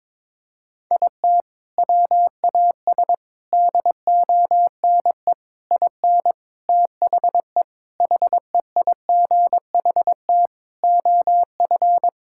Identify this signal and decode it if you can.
Morse code, heard as a keyed tone, about 22 wpm, 700 Hz. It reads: IT WAS DONE IN THE HEIGHT OF
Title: Sentences from Top 1600 Words 22wpm